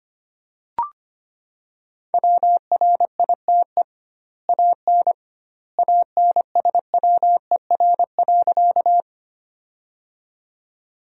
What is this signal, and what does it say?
Morse code, heard as a keyed tone, about 25 wpm, 700 Hz. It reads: WRITE AN ANSWER.
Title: 2x Word-Spacing - Sentences from Top 300 Words - 25wpm